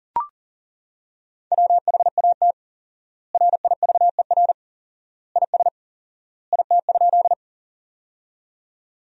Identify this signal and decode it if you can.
Morse code, heard as a keyed tone, about 37 wpm, 700 Hz. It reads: WHAT RIVER IS IT?
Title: 4x Word-Spacing - Sentences from Top 300 Words - 40wpm